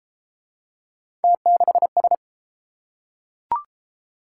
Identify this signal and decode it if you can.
Morse code, heard as a keyed tone, about 29 wpm, 700 Hz. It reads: T6S